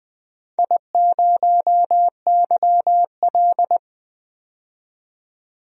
Morse code, heard as a keyed tone, about 21 wpm, 700 Hz. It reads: I0YL
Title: DXCC-100 Call Signs - 4 Characters - 20wpm